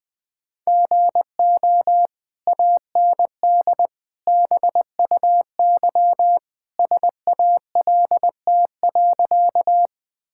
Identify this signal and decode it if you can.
Morse code, heard as a keyed tone, about 19 wpm, 700 Hz. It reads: GO AND BUY SALT.
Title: Sentences from Top 1000 Words - Review Part 5 - 20wpm